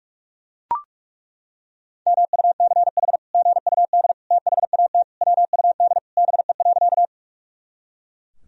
Morse code, heard as a keyed tone, about 42 wpm, 700 Hz. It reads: MUCH GUD THAT WUD BE.
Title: Sentences from Top 200 Words and Abbreviations 45wpm